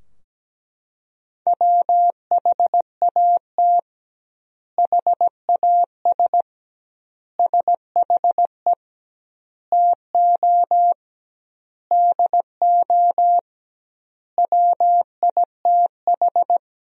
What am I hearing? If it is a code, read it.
Morse code, heard as a keyed tone, about 17 wpm, 700 Hz. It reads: WHAT HAS SHE TO DO WITH
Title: 2x Word-Spacing - Sentences from Top 100 Words - 17wpm